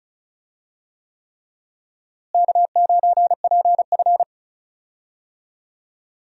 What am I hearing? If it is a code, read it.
Morse code, heard as a keyed tone, about 34 wpm, 700 Hz. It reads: K9PF
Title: USA Call Signs - 4 Characters - Repeated 3x - 35wpm